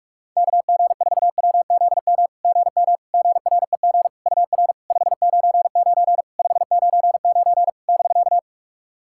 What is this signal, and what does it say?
Morse code, heard as a keyed tone, about 48 wpm, 690 Hz. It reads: KG4WZM GM GREG UR 599 599 <BK>
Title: SOTA QSO Practice 45wpm